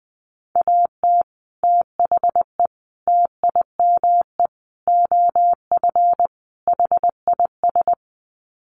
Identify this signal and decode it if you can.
Morse code, heard as a keyed tone, about 20 wpm, 700 Hz. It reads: AT THE TIME OF HIS
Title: Sets of 5 Words Encore2 20wpm